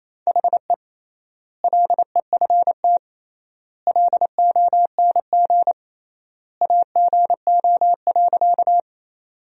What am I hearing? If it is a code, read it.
Morse code, heard as a keyed tone, about 28 wpm, 700 Hz. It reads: HE LEFT LONG AGO.
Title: 3x Word-Spacing - Sentences from Top 500 Words - 28wpm